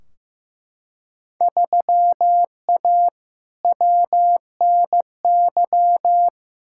Morse code, heard as a keyed tone, about 15 wpm, 700 Hz. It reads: 3A WNY
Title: Field Day Exchanges - 15wpm